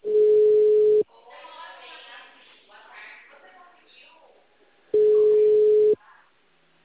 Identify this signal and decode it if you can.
Ringback tone (Europe, 425 Hz), with speech.
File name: out- -2016- - - .73